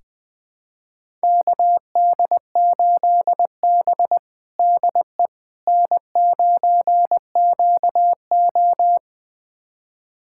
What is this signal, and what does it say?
Morse code, heard as a keyed tone, about 20 wpm, 700 Hz. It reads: KD8B DE N9QO
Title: USA Callsign DE Callsign - 4 Characters - 20wpm